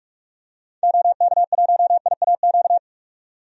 Morse code, heard as a keyed tone, about 48 wpm, 700 Hz. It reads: OK1IAQ
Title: DXCC-100 Call Signs - 6 Characters - 45wpm